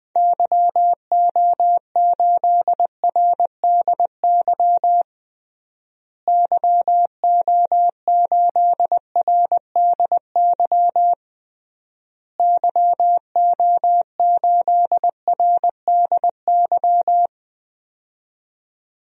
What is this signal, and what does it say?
Morse code, heard as a keyed tone, about 20 wpm, 700 Hz. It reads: YO8RDY YO8RDY YO8RDY
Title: DXCC-100 Call Signs - 6 Characters - Repeated 3x - 20wpm